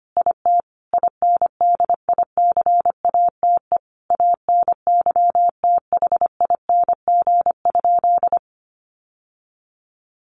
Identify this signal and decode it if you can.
Morse code, heard as a keyed tone, about 25 wpm, 700 Hz. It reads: IT INDICATE ANYTHING?
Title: Sentences from Top 1300 Words 25wpm